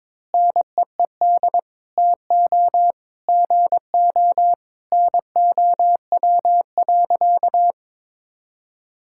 Morse code, heard as a keyed tone, about 22 wpm, 700 Hz. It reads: NEED TO GO NOW.
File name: practice / 22wpm / Sentences with 2-Word Phrases - 22wpm